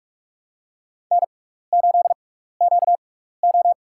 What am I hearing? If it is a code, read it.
Morse code, heard as a keyed tone, about 46 wpm, spaced as for 16 wpm, 700 Hz. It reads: N8QO
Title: USA Call Signs - 6x Character Spacing - 45wpm